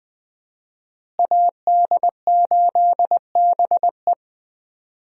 Morse code, heard as a keyed tone, about 20 wpm, 700 Hz. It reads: AD8BE
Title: USA Call Signs - 5 Characters - Repeated 2x - 20wpm